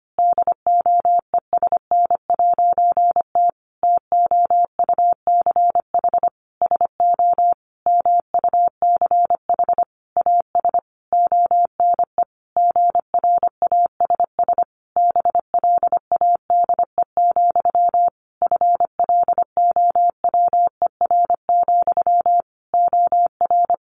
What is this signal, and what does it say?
Morse code, heard as a keyed tone, about 25 wpm, 700 Hz. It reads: DOESN'T TOUCH SO MUCH AS ONE GRASS BLADE, FLOWER, OR
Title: A Sound Of Thunder 25wpm